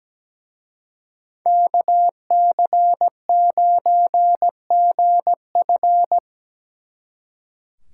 Morse code, heard as a keyed tone, about 20 wpm, 700 Hz. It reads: KC9GF
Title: Speed Racing - USA 2x2 Call Signs 17wpm